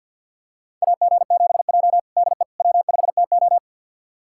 Morse code, heard as a keyed tone, about 51 wpm, 700 Hz. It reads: AG7J DE W5TO
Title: USA Callsign DE Callsign - 4 Characters - 50wpm